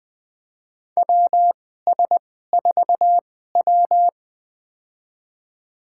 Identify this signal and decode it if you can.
Morse code, heard as a keyed tone, about 20 wpm, spaced as for 14 wpm, 700 Hz. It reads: WS4W
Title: USA Call Signs - 2x Character Spacing - 20wpm